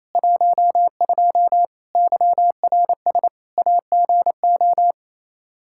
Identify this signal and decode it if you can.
Morse code, heard as a keyed tone, about 28 wpm, 700 Hz. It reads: 12 YRS AGO
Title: Practice QSO - Exchange 3 - More Info - 28wpm